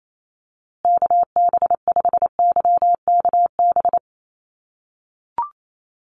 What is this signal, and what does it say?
Morse code, heard as a keyed tone, about 28 wpm, 700 Hz. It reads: KB5YKB
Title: USA Call Signs Repeated 2x 28wpm